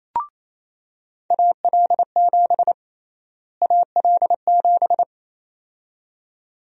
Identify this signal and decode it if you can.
Morse code, heard as a keyed tone, about 28 wpm, 700 Hz. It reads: AL7 AL7
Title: USA Call Sign Prefixes - Repeated 2x - 28wpm